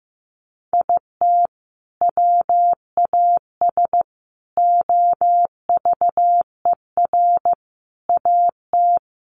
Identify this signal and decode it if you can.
Morse code, heard as a keyed tone, about 14 wpm, 700 Hz. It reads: IT WAS OVER AT
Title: Sentences from Top 600 Words 15wpm